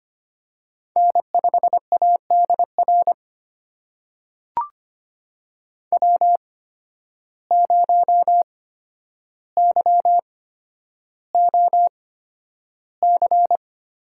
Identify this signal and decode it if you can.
Morse code, heard as a keyed tone, about 25 wpm, spaced as for 11 wpm, 700 Hz. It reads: N5ADR W0YOC
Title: USA Call Signs - 5 Characters - 8x Character Spacing - 25wpm